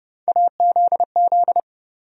Morse code, heard as a keyed tone, about 30 wpm, 700 Hz. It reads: AZZ
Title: USA Call Sign Postfixes - 3 Letters 30wpm